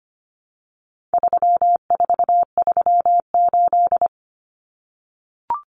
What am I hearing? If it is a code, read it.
Morse code, heard as a keyed tone, about 25 wpm, 700 Hz. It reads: 3438
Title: Numbers - 4-Digit 25wpm